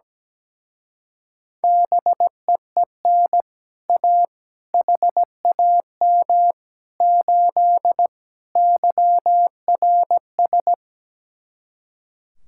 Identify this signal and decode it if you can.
Morse code, heard as a keyed tone, about 17 wpm, 700 Hz. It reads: BEEN A HAM 8 YRS
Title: Speed-Racing Common QSO Phrases - 17wpm